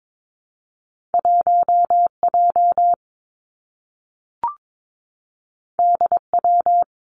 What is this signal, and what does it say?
Morse code, heard as a keyed tone, about 22 wpm, 700 Hz. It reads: 1 J D W
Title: Two Challenging Letter-Number Combinations 22wpm